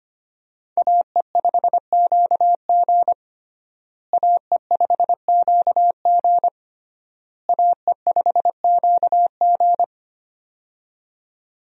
Morse code, heard as a keyed tone, about 25 wpm, 700 Hz. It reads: AE5QG AE5QG AE5QG
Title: USA Call Signs - 5 Characters - Repeated 3x - 25wpm